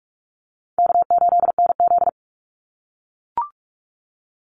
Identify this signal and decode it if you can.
Morse code, heard as a keyed tone, about 44 wpm, 700 Hz. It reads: K8NZ
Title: Easy Call Signs - 45wpm